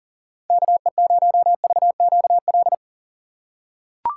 Morse code, heard as a keyed tone, about 40 wpm, 700 Hz. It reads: KE0VQL
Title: USA Call Signs - 4x Character Spacing - 40wpm